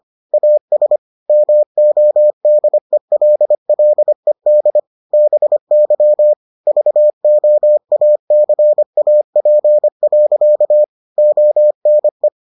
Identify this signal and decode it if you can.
Morse code, heard as a keyed tone, about 25 wpm, 600 Hz. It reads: AS MODELLED BY VOACAP. ONE